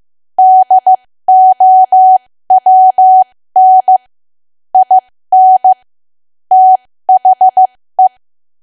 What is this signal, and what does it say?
Morse code, heard as a keyed tone, about 15 wpm, spaced as for 13 wpm, 750 Hz. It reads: DOWN IN THE
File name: cwa / W1AW-Level2 / 13WPM